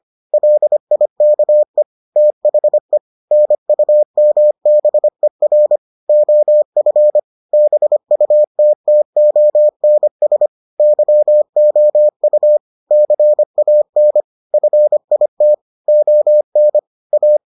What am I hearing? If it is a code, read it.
Morse code, heard as a keyed tone, about 25 wpm, 600 Hz. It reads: LIKE THE NUMBER OF BUTTONS YOU CAN FIT ON A